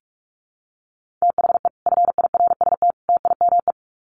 Morse code, heard as a keyed tone, about 44 wpm, 700 Hz. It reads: THE FIRST TIME
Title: Sets of 3 Words 45wpm